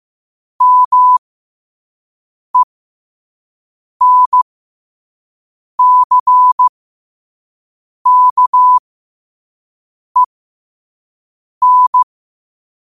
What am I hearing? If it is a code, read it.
Morse code, heard as a keyed tone, about 15 wpm, spaced as for 5 wpm, 1000 Hz. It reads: MENCKEN
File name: qotdfiles / 5wpm